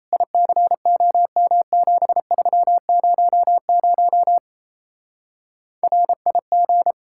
Morse code, heard as a keyed tone, about 35 wpm, 700 Hz. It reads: ICOM7300 RIG
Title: Speed-Racing Common QSO Phrases - 28wpm